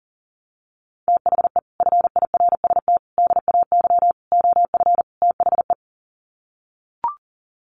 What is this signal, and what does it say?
Morse code, heard as a keyed tone, about 37 wpm, 700 Hz. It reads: THE FIRST DAY OF THE
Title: Sets of 5 Words Encore2 40wpm